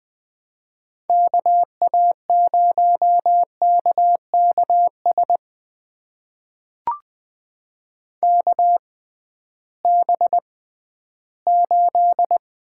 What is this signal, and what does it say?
Morse code, heard as a keyed tone, about 21 wpm, spaced as for 13 wpm, 700 Hz. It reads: KA0KKS KB8
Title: USA Call Signs - 6 Characters - 6x Character Spacing - 20wpm